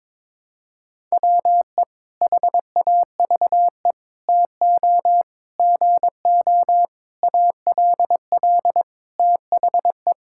Morse code, heard as a keyed tone, about 21 wpm, 700 Hz. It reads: WE HAVE TO GO ALL THE
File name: practice / 22wpm / Sentences with 8-Word Phrases - 22wpm